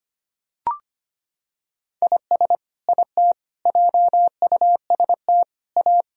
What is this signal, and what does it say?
Morse code, heard as a keyed tone, about 25 wpm, 700 Hz. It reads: IS IT JUST A
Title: Sentences from Top 1000 Words - Review Part 2 - 25wpm